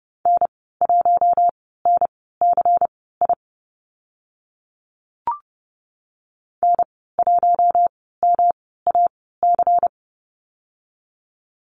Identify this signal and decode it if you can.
Morse code, heard as a keyed tone, about 30 wpm, spaced as for 17 wpm, 700 Hz. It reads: N1NCI N1MAC
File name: Lesson 19g - Letter w - Call Signs 30wpm